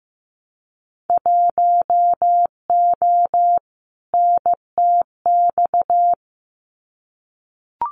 Morse code, heard as a keyed tone, about 15 wpm, 700 Hz. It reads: 1O NTX
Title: Winter Field Day Exchanges 15wpm